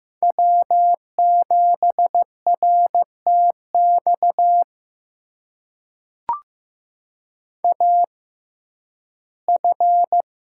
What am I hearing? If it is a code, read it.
Morse code, heard as a keyed tone, about 15 wpm, spaced as for 11 wpm, 700 Hz. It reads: W7RTX AF
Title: USA Call Signs - 5 Characters - 6x Character Spacing - 15wpm